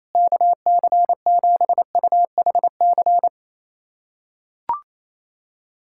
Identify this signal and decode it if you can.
Morse code, heard as a keyed tone, about 27 wpm, 700 Hz. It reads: KC7UHC
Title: USA Callsign DE Callsign - 6 Characters - 28wpm